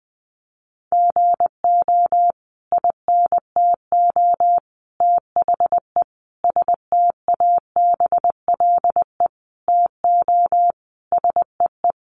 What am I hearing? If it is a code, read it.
Morse code, heard as a keyed tone, about 20 wpm, 700 Hz. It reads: GO INTO THE STABLE TO SEE
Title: Sentences from Top 1500 Words 20wpm